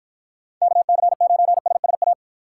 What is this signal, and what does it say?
Morse code, heard as a keyed tone, about 50 wpm, 700 Hz. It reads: KC9SSA